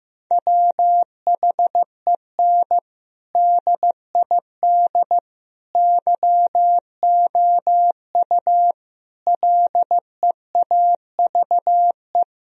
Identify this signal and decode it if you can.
Morse code, heard as a keyed tone, about 15 wpm, 700 Hz. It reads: WHEN DID YOU LEAVE
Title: Sentences from Top 1000 Words - Review Part 4 - 15wpm